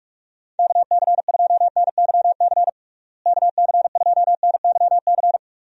Morse code, heard as a keyed tone, about 46 wpm, 700 Hz. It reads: KC2NYC KC2NYC
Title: USA Call Signs - 6 Characters - Repeated 3x - 45wpm